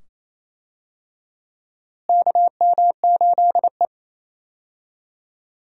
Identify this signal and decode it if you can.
Morse code, heard as a keyed tone, about 28 wpm, 700 Hz. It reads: KM8E
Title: USA Call Signs - 4 Characters - Repeated 2x - 28wpm